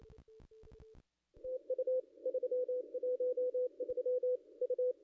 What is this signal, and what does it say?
Morse code, heard as a keyed tone, about 28 wpm, 500 Hz. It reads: TU 313 U
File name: contests / ss-cw-k5tr-op-n5rz -0200